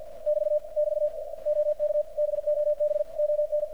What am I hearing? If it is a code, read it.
Morse code, heard as a keyed tone, about 50 wpm, 600 Hz, with radio noise. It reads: <BT> <BT> OK BOB ON
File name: archived / CRSnet